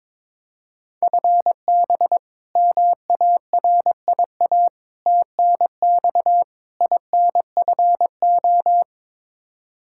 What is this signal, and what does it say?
Morse code, heard as a keyed tone, about 22 wpm, 700 Hz. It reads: FB MARIA TNX INFO
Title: K1USN SST QSO Practice 22wpm